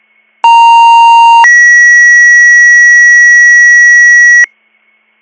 Two-tone page, heard as a one-second tone, then a three-second tone, about 930 then 1950 Hz. It is heard from 0.4 to 4.4 s.